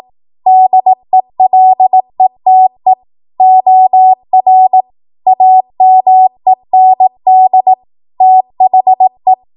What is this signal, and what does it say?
Morse code, heard as a keyed tone, about 18 wpm, 750 Hz. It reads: DELETE OR AMEND THE
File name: Morse / 18WPM